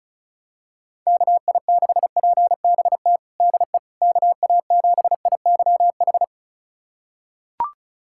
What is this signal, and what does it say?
Morse code, heard as a keyed tone, about 36 wpm, 700 Hz. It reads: KI6PBT DE KA7IYH
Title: USA Callsign DE Callsign - 6 Characters - 35wpm